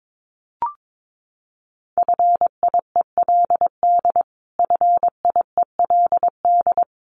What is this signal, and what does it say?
Morse code, heard as a keyed tone, about 22 wpm, 700 Hz. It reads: FIELD FIELD
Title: Top 500 Words Repeated 22wpm